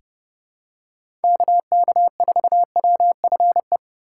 Morse code, heard as a keyed tone, about 30 wpm, 700 Hz. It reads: KK4WFE
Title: USA Call Signs - 6 Characters - 2x Character Spacing - 30wpm